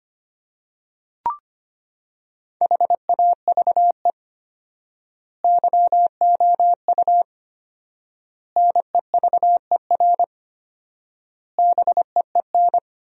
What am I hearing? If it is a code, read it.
Morse code, heard as a keyed tone, about 25 wpm, 700 Hz. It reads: HAVE YOU NEVER BEEN
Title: 4x Word-Spacing - Sentences from Top 300 Words - 25wpm